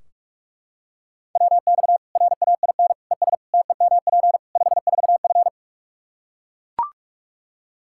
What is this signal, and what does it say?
Morse code, heard as a keyed tone, about 42 wpm, 700 Hz. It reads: WX RAIN ES TEMP 54F
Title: Practice QSO - Exchange 2 - Info - 45wpm